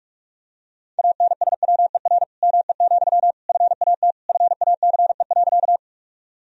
Morse code, heard as a keyed tone, about 46 wpm, 700 Hz. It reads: ANSWER ME, FAT FACE.
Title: Sentences from Top 1000 Words - Review Part 4 - 45wpm